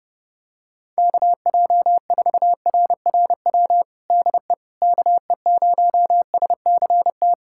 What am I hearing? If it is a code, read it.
Morse code, heard as a keyed tone, about 30 wpm, 700 Hz. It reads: KJ4RRW DE KE0SCT
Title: USA Callsign DE Callsign - 6 Characters - 30wpm